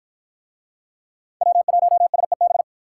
Morse code, heard as a keyed tone, about 53 wpm, 700 Hz. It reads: W1SEB